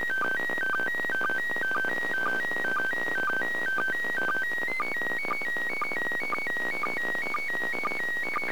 SSTV (slow-scan television), PD 120.
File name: 2020 0930 ISS SSTV